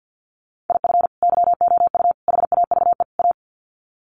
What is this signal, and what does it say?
Morse code, heard as a keyed tone, about 50 wpm, 700 Hz. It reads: IF YOU HAVE A
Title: Sets of 4 Words 50wpm